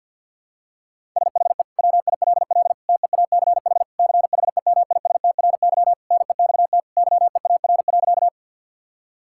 Morse code, heard as a keyed tone, about 52 wpm, 700 Hz. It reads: SHE WILL TEACH CHEMISTRY NEXT YEAR.
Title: Sentences from Top 1000 Words - Review Part 6 - 50wpm